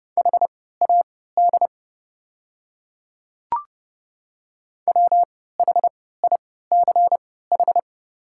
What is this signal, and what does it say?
Morse code, heard as a keyed tone, about 30 wpm, spaced as for 17 wpm, 690 Hz. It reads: HAD WHICH
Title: 3x Character Spacing - Top 100 Words - 30wpm